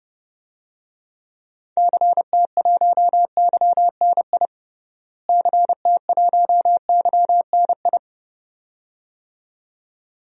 Morse code, heard as a keyed tone, about 30 wpm, 700 Hz. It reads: CT1YNI CT1YNI
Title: DXCC-100 Call Signs - 6 Characters - Repeated 2x - 30wpm